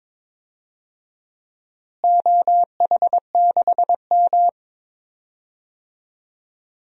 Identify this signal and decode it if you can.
Morse code, heard as a keyed tone, about 22 wpm, 700 Hz. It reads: OH6M
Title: DXCC-100 Call Signs - 4 Characters - Repeated 3x - 22wpm